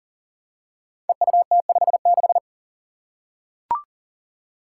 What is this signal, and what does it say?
Morse code, heard as a keyed tone, about 41 wpm, 700 Hz. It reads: EUT56
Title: Code Groups - 5 Letter:Numbers - 40wpm